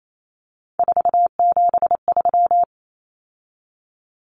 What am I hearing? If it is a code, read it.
Morse code, heard as a keyed tone, about 28 wpm, 700 Hz. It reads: 473
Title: Numbers - 3-Digit 28wpm